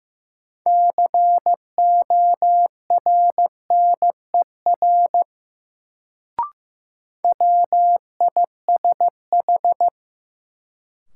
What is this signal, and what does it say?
Morse code, heard as a keyed tone, about 15 wpm, 700 Hz. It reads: CORNER WISH
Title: Sentences from Top 1600 Words - New Word Drill - 15wpm